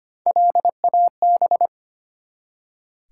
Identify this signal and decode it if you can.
Morse code, heard as a keyed tone, about 25 wpm, 700 Hz. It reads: LAB
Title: Three Letter Words - 25wpm